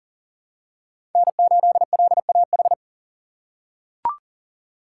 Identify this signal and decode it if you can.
Morse code, heard as a keyed tone, about 37 wpm, 700 Hz. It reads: N8LAH
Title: USA Call Signs - 5 Characters - 4x Character Spacing - 40wpm